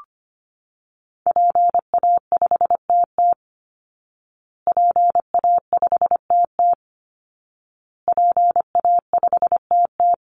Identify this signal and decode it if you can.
Morse code, heard as a keyed tone, about 25 wpm, 700 Hz. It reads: PA5TT PA5TT PA5TT
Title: Call Signs Repeated 2x - 25wpm